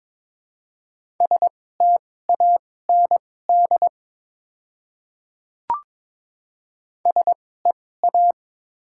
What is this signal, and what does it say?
Morse code, heard as a keyed tone, about 22 wpm, spaced as for 16 wpm, 700 Hz. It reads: STAND SEA
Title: 2x Character Spacing - Top 300 Words - 22wpm